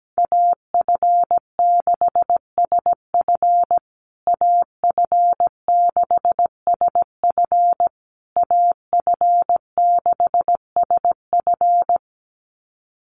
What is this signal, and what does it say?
Morse code, heard as a keyed tone, about 17 wpm, 700 Hz. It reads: AF6SF AF6SF AF6SF
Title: USA Call Signs Repeated 3x 17wpm